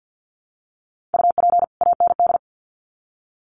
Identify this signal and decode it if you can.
Morse code, heard as a keyed tone, about 50 wpm, 700 Hz. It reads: UP AND
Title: Sets of 2 Words 50wpm